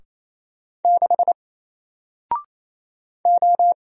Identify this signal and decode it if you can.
Morse code, heard as a keyed tone, about 28 wpm, 700 Hz. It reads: 6 O
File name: Single-Letter-Number-28wpm